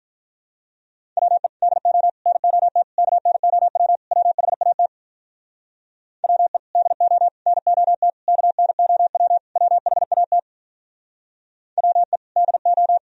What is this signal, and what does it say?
Morse code, heard as a keyed tone, about 47 wpm, 700 Hz. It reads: WE DO NOT KNOW WHAT WE DO NOT KNOW WHAT WE DO